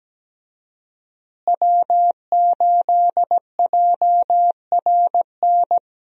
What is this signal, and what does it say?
Morse code, heard as a keyed tone, about 17 wpm, 700 Hz. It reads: W8JRN
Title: USA Call Signs - 5 Characters - Repeated 3x - 17wpm